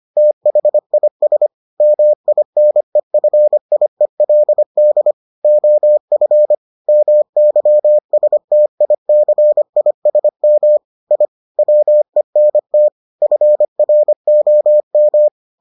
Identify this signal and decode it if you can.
Morse code, heard as a keyed tone, about 25 wpm, 600 Hz. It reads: THIS MINEFIELD OF MYSTICISM I WENT FROM